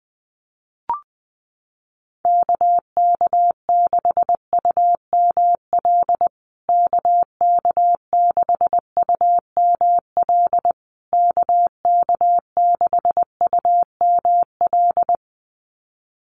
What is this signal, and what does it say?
Morse code, heard as a keyed tone, about 20 wpm, 700 Hz. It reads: KK6UML KK6UML KK6UML
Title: USA Call Signs Repeated 3x 20wpm